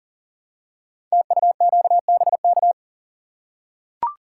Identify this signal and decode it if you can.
Morse code, heard as a keyed tone, about 39 wpm, 700 Hz. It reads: TUQBK
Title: Code Groups - 5 Letters - 40wpm